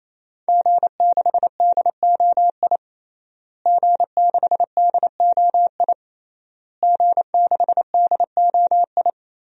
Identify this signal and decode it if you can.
Morse code, heard as a keyed tone, about 28 wpm, 700 Hz. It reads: G6DOI G6DOI G6DOI
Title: DXCC-100 Call Signs - 5 Characters - Repeated 3x - 28wpm